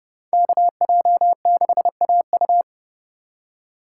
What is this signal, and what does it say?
Morse code, heard as a keyed tone, about 28 wpm, 700 Hz. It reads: KJ6AU